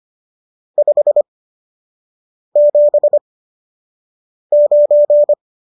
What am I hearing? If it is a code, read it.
Morse code, heard as a keyed tone, about 25 wpm, spaced as for 6 wpm, 590 Hz. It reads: 579